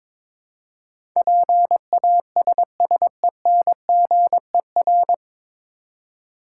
Morse code, heard as a keyed tone, about 22 wpm, 700 Hz. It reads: PASSENGER
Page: Morse Code Ninja practice 22wpm